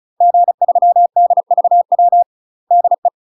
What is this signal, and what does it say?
Morse code, heard as a keyed tone, about 36 wpm, 700 Hz. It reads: G3DVW DE
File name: UK MASTER QSOs TRAINING / 35WPM / UK MASTER QSOs 0022